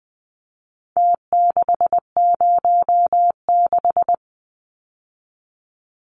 Morse code, heard as a keyed tone, about 20 wpm, 700 Hz. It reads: T606
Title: Numbers - Contest Style - 4-Digit 20wpm